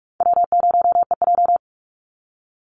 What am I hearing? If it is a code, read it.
Morse code, heard as a keyed tone, about 44 wpm, 700 Hz. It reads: W0EJ
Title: Easy Call Signs - 45wpm